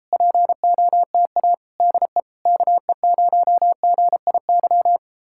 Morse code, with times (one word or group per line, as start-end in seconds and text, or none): 0.1-1.5 POTA
1.8-2.2 DE
2.5-5.0 KE0GIY